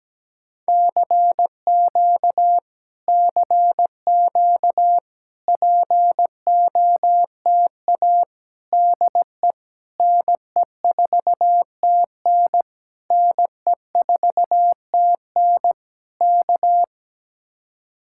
Morse code, heard as a keyed tone, about 17 wpm, 700 Hz. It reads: CQ CQ POTA DE NE4TN NE4TN K
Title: Speed-Racing Common QSO Phrases - 17wpm